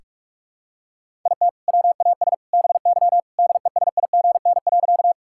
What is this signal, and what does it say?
Morse code, heard as a keyed tone, about 46 wpm, 700 Hz. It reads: IT WAS BY DESIGN.
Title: Sentences from Top 1000 Words - Review Part 5 - 45wpm